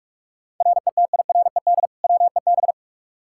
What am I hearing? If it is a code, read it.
Morse code, heard as a keyed tone, about 41 wpm, 700 Hz. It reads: RETIRED WEB
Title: K1USN SST QSO Practice 45wpm